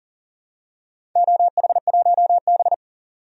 Morse code, heard as a keyed tone, about 40 wpm, 700 Hz. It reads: OH1B
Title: DXCC-100 Call Signs - 4 Characters - 40wpm